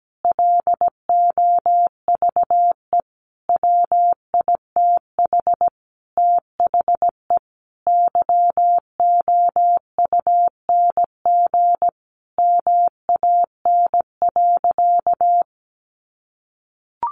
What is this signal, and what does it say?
Morse code, heard as a keyed tone, about 17 wpm, 700 Hz. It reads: LOVE WITH THE YOUNG MAN.
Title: Sentences from Top 900 Words 17wpm